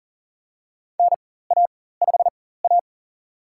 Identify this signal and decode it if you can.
Morse code, heard as a keyed tone, about 40 wpm, spaced as for 19 wpm, 700 Hz. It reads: NA5A
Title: USA Call Signs - 4x Character Spacing - 40wpm